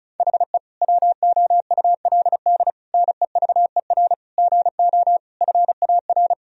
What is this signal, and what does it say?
Morse code, heard as a keyed tone, about 36 wpm, 700 Hz. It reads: HE WOULD NEVER GO FAR
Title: Sentences from Top 1000 Words - Review Part 2 - 35wpm